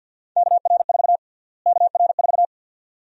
Morse code, heard as a keyed tone, about 52 wpm, 700 Hz. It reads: KR4 KR4
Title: USA Call Sign Prefixes - Repeated 2x - 50wpm